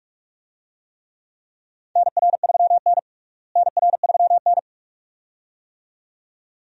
Morse code, heard as a keyed tone, about 46 wpm, 700 Hz. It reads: NR3N NR3N
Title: USA Call Signs - 4 Characters - Repeated 2x - 45wpm